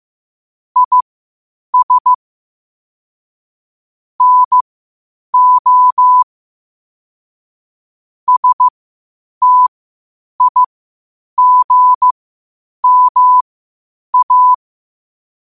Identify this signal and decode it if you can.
Morse code, heard as a keyed tone, about 15 wpm, spaced as for 8 wpm, 1000 Hz. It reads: IS NO STIGMA
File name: qotdfiles / 8wpm